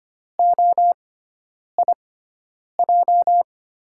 Morse code, heard as a keyed tone, about 25 wpm, spaced as for 9 wpm, 700 Hz. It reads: OIJ
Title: USA Call Signs - 6 Characters - 6x Character Spacing - 25wpm